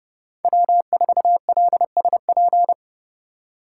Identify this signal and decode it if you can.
Morse code, heard as a keyed tone, about 31 wpm, 700 Hz. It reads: W4LSP
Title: USA Call Signs - 5 Characters - 3x Character Spacing - 30wpm